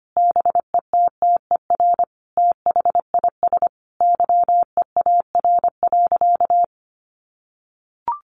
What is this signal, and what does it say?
Morse code, heard as a keyed tone, about 25 wpm, 700 Hz. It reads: BETTER THIS YEAR.
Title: Sentences from Top 1500 Words 25wpm